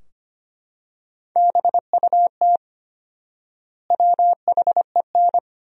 Morse code, heard as a keyed tone, about 25 wpm, 700 Hz. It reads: BUT WHEN
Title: 4x Word-Spacing - Sentences from Top 100 Words - 25wpm